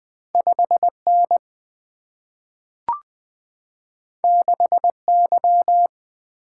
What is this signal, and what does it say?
Morse code, heard as a keyed tone, about 20 wpm, 700 Hz. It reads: 5N 6Y
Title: Two Letter-Number Combinations - 20wpm